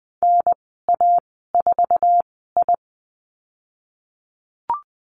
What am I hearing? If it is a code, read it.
Morse code, heard as a keyed tone, about 20 wpm, spaced as for 15 wpm, 700 Hz. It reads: NA4I
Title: 088 Lesson 15g - Number 5 - Call Signs 20wpm